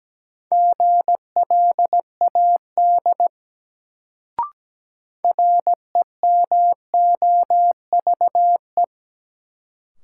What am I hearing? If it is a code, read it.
Morse code, heard as a keyed tone, about 17 wpm, 700 Hz. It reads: GLAD REMOVE
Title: Sentences from Top 800 Words - New Word Drill - 17wpm